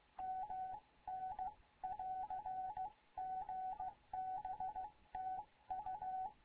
Morse code, heard as a keyed tone, about 16 wpm, 750 Hz. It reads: MN <AR> GB TU
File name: Morze-HA2MN / hangp / 2 11 GYAKORLAT